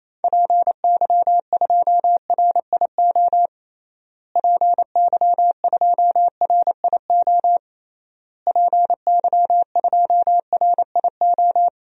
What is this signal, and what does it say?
Morse code, heard as a keyed tone, about 28 wpm, 700 Hz. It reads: PY2RIO PY2RIO PY2RIO
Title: DXCC-100 Call Signs - 6 Characters - Repeated 3x - 28wpm